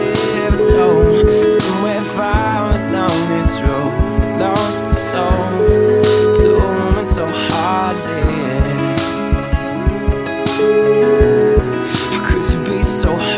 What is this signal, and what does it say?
Ringback tone (Europe, 425 Hz), with noise.